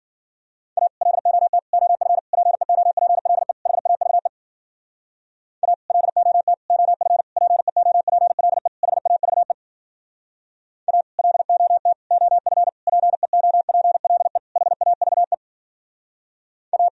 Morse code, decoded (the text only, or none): A LOT OF PEOPLE HAVE A LOT OF PEOPLE HAVE A LOT OF PEOPLE HAVE A